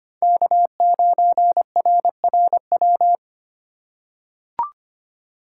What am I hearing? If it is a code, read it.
Morse code, heard as a keyed tone, about 25 wpm, 700 Hz. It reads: K9RRW
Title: USA Call Signs - 8x Character Spacing - 25wpm